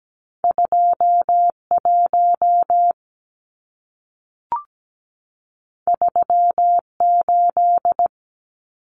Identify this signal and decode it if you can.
Morse code, heard as a keyed tone, about 17 wpm, 700 Hz. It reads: 21 38
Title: Numbers - 2-Digit 17wpm